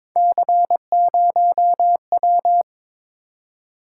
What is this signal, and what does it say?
Morse code, heard as a keyed tone, about 23 wpm, 700 Hz. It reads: C0W
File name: practice / 22wpm / Three Letter-Number Combinations - 22wpm